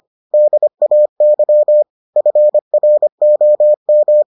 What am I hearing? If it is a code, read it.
Morse code, heard as a keyed tone, about 25 wpm, 600 Hz. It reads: DAY FROM